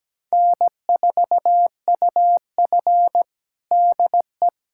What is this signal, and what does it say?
Morse code, heard as a keyed tone, about 17 wpm, 700 Hz. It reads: N4UF DE
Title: USA Callsign DE Callsign - 4 Characters - 17wpm